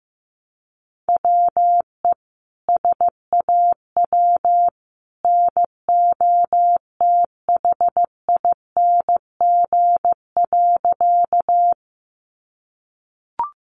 Morse code, heard as a keyed tone, about 15 wpm, 700 Hz. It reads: WE SAW NOTHING.
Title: A Sound Of Thunder 15wpm